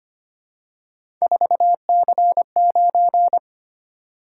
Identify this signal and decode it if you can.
Morse code, heard as a keyed tone, about 25 wpm, 700 Hz. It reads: 4C9
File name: Three Letter-Number Combinations - 25wpm